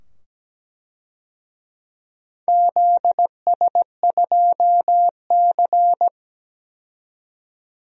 Morse code, heard as a keyed tone, about 17 wpm, 700 Hz. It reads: ZS2C
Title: DXCC-100 Call Signs - 4 Characters - 17wpm